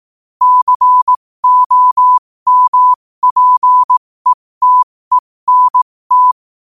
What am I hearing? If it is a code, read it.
Morse code, heard as a keyed tone, about 18 wpm, spaced as for 15 wpm, 1000 Hz. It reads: COMPETENT
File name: qotdfiles / 15wpm